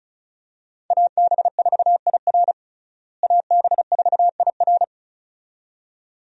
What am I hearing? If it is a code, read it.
Morse code, heard as a keyed tone, about 34 wpm, 700 Hz. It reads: AB4IR AB4IR
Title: USA Call Signs - 5 Characters - Repeated 2x - 35wpm